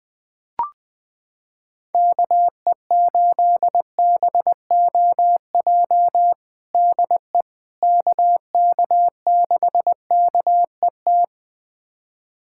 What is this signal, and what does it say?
Morse code, heard as a keyed tone, about 20 wpm, 700 Hz. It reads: KE8BOJ DE KK6KET
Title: USA Callsign DE Callsign - 6 Characters - 20wpm